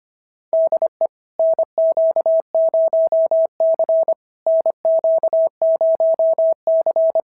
Morse code, heard as a keyed tone, about 25 wpm, 690 Hz. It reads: DE NQ0C NQ0C
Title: SOTA QSO Practice 25wpm